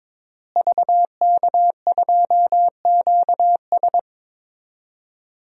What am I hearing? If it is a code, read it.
Morse code, heard as a keyed tone, about 22 wpm, 700 Hz. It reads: VK2QS
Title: DXCC-100 Call Signs - 5 Characters - Repeated 2x - 22wpm